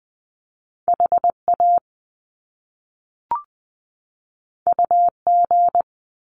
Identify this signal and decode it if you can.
Morse code, heard as a keyed tone, about 20 wpm, 700 Hz. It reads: HA UG
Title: Top Two Letter Combinations in Words 20wpm